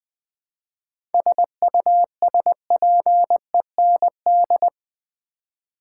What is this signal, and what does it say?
Morse code, heard as a keyed tone, about 20 wpm, 700 Hz. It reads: SUSPEND
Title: 7 Letter Words - First 3 Letters Given - 20wpm